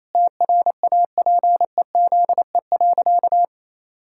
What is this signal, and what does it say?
Morse code, heard as a keyed tone, about 27 wpm, 700 Hz. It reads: TRAPEZE.
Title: Pangram Challenge v2 - 28wpm